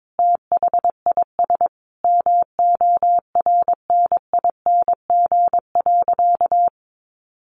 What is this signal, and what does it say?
Morse code, heard as a keyed tone, about 22 wpm, 700 Hz. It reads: THIS MORNING.
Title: Sentences from Top 1500 Words 22wpm